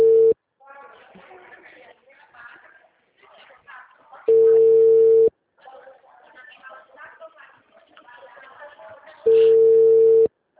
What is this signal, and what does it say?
Ringback tone (China, 450 Hz), with speech.